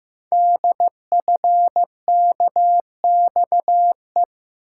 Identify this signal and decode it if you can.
Morse code, heard as a keyed tone, about 15 wpm, 700 Hz. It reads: DFKXE
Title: Code Groups - 5 Letters - 15wpm